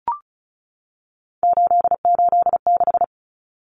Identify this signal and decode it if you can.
Morse code, heard as a keyed tone, about 34 wpm, 700 Hz. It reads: 886
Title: Numbers - 3-Digit 35wpm